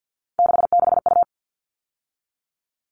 Morse code, heard as a keyed tone, about 50 wpm, 700 Hz. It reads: 66U
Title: All Cut Numbers - 3-Digit 50wpm